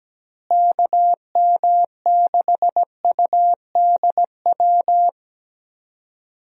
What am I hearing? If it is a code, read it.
Morse code, heard as a keyed tone, about 17 wpm, 700 Hz. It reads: KM6UDW
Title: USA Call Signs - 6 Characters - Repeated 3x - 17wpm